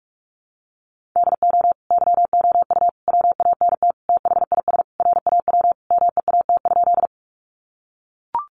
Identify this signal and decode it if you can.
Morse code, heard as a keyed tone, about 44 wpm, 700 Hz. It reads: DO YOU WANT THIS RAW MEAT?
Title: Sentences from Top 1500 Words 45wpm